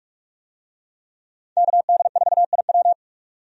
Morse code, heard as a keyed tone, about 42 wpm, 700 Hz. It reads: KD4IW
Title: USA Call Signs - 5 Characters - 45wpm